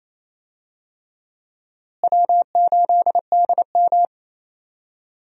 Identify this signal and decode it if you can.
Morse code, heard as a keyed tone, about 28 wpm, 700 Hz. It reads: W8DM
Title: USA Call Signs - 4 Characters - Repeated 3x - 28wpm